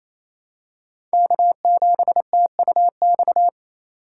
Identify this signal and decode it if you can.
Morse code, heard as a keyed tone, about 28 wpm, 700 Hz. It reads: K7TUX
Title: USA Call Signs - 5 Characters - 4x Character Spacing - 28wpm